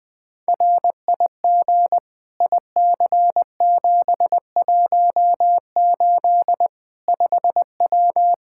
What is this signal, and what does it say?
Morse code, heard as a keyed tone, about 20 wpm, 700 Hz. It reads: RIG IC718 5W
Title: K1USN SST QSO Practice 20wpm